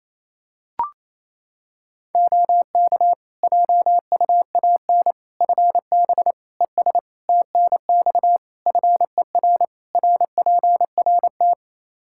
Morse code, heard as a keyed tone, about 28 wpm, 700 Hz. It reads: OK JUAN FB ES TNX FER RPRT
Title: K1USN SST QSO Practice 28wpm